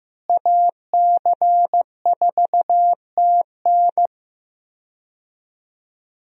Morse code, heard as a keyed tone, about 15 wpm, 700 Hz. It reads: AC4TN
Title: USA Call Signs - 5 Characters - Repeated 3x - 15wpm